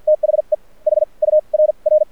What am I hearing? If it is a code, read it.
Morse code, heard as a keyed tone, about 50 wpm, spaced as for 37 wpm, 610 Hz. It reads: THE HURR
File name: archived / CRSnet